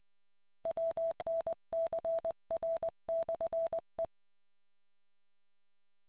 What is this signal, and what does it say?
Morse code, heard as a keyed tone, about 20 wpm, 660 Hz. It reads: WNCR/E